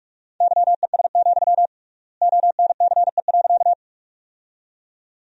Morse code, heard as a keyed tone, about 46 wpm, 700 Hz. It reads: YES, ONCE.
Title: 3x Word-Spacing - Sentences from Top 500 Words - 45wpm